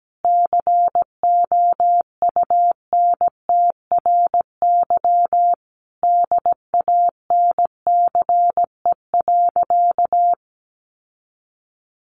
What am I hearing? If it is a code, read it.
Morse code, heard as a keyed tone, about 17 wpm, 700 Hz. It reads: COUNTRY DANCE.
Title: Sentences from Top 1300 Words 17wpm